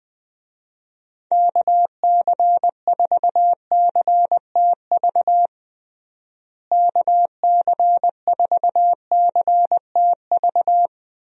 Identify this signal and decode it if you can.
Morse code, heard as a keyed tone, about 19 wpm, 700 Hz. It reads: KC4CTV KC4CTV
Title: USA Call Signs - 6 Characters - Repeated 2x - 20wpm